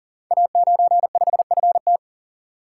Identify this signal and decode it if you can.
Morse code, heard as a keyed tone, about 38 wpm, 700 Hz. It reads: A95FT
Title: Code Groups - 5 Letter:Numbers - 40wpm